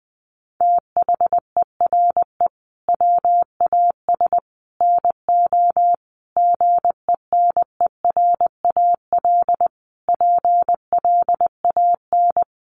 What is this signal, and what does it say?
Morse code, heard as a keyed tone, about 19 wpm, 700 Hz. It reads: THERE WAS NO GENERAL PLAN
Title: Sentences from Top 700 Words 20wpm